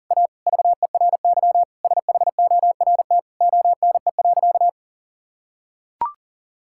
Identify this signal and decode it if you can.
Morse code, heard as a keyed tone, about 40 wpm, 700 Hz. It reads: A VERY SHORT ONE.
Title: Sentences from Top 1600 Words 40wpm